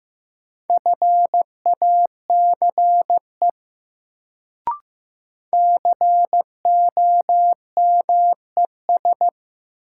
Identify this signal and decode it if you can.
Morse code, heard as a keyed tone, about 16 wpm, 700 Hz. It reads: FACE COMES
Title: Sets of 3 Words - New Word Drill - 15wpm